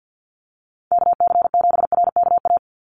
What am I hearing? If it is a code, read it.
Morse code, heard as a keyed tone, about 50 wpm, 700 Hz. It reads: KC7RKA
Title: USA Call Signs 50wpm